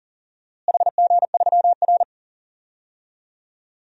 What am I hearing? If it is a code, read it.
Morse code, heard as a keyed tone, about 40 wpm, 700 Hz. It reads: HG3R
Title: DXCC-100 Call Signs - 4 Characters - Repeated 2x - 40wpm